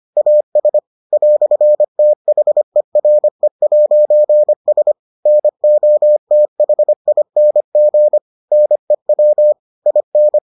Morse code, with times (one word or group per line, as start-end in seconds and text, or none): 0.2-0.8 AS
1.1-4.9 "THERE'S
5.3-8.2 NOTHING
8.5-9.5 NEW
9.9-10.4 IN